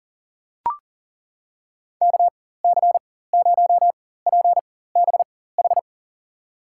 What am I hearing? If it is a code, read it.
Morse code, heard as a keyed tone, about 40 wpm, spaced as for 19 wpm, 700 Hz. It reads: KC0PBH
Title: USA Call Signs - 6 Characters - 4x Character Spacing - 40wpm